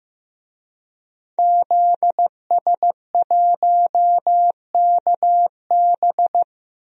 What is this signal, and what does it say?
Morse code, heard as a keyed tone, about 15 wpm, 700 Hz. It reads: ZS1KB
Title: DXCC-100 Call Signs - 5 Characters - Repeated 2x - 15wpm